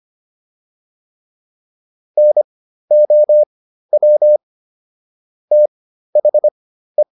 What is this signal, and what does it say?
Morse code, heard as a keyed tone, about 25 wpm, spaced as for 13 wpm, 600 Hz. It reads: NOW THE